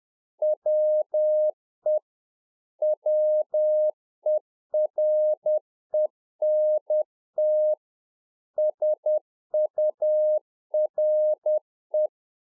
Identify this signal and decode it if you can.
Morse code, heard as a keyed tone, about 10 wpm, 610 Hz. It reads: WE WERENT SURE